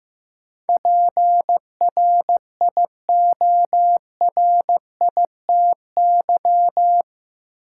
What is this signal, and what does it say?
Morse code, heard as a keyed tone, about 15 wpm, 700 Hz. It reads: PRIORITY
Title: Sentences from Top 1600 Words - New Word Drill Alt - 15wpm